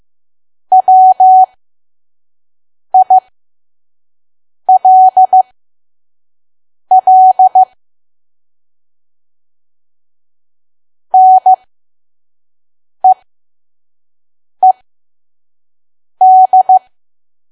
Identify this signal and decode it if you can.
Morse code, heard as a keyed tone, about 15 wpm, spaced as for 5 wpm, 750 Hz. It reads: WILL NEED
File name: Archive / 5 WPM files / PM